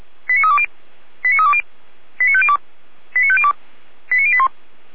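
Five-tone selective call (ZVEI), heard as quick runs of five tones, tones of about 70 ms. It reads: 89320 89320 89502 89502 89081
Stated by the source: ZVEI five-tone selective call 89340 received by SDR.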